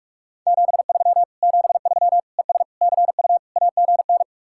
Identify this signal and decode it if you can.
Morse code, heard as a keyed tone, about 56 wpm, 700 Hz. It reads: 73 73 ES CU AGN
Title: Speed-Racing Common QSO Phrases - 45wpm